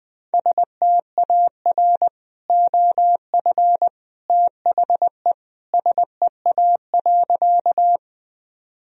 Morse code, heard as a keyed tone, about 20 wpm, 700 Hz. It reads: STAR OF THE SEA.
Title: Sentences from Top 1000 Words - Review Part 2 - 20wpm